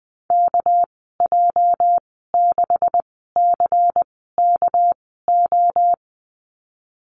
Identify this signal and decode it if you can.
Morse code, heard as a keyed tone, about 20 wpm, spaced as for 15 wpm, 700 Hz. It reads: KJ6CKO
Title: 238 Lesson 38g - Letter q - Call Signs 20wpm